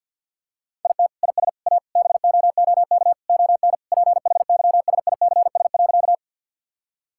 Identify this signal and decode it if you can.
Morse code, heard as a keyed tone, about 52 wpm, 700 Hz. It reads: IT IS A BOOK ON PHYSICS.
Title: Sentences from Top 1000 Words - Review Part 5 - 50wpm